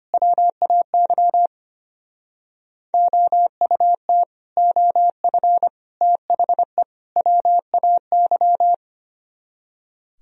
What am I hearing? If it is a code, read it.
Morse code, heard as a keyed tone, about 32 wpm, 700 Hz. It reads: WAY OUT OF THE WAY